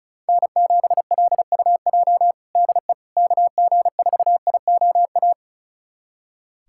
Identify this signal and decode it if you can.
Morse code, heard as a keyed tone, about 36 wpm, 700 Hz. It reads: N7LUJ DE KG4IOA
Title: USA Callsign DE Callsign 35wpm